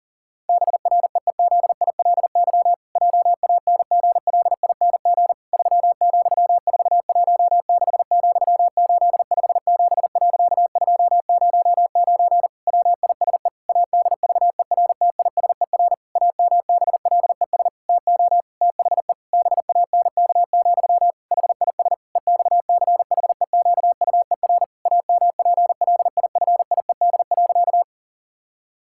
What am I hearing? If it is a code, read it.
Morse code, heard as a keyed tone, about 40 wpm, 700 Hz. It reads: BREEZILY JANGLING 3,416,857.209 WISE ADVERTISER AMBLES TO THE BANK, HIS EXCHEQUER AMPLIFIED.
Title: Pangram Challenge v2 - 40wpm